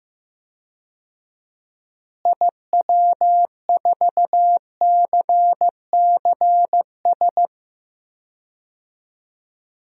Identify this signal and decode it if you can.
Morse code, heard as a keyed tone, about 15 wpm, 700 Hz. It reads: IW4CCS
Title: DXCC-100 Call Signs - 6 Characters - Repeated 3x - 15wpm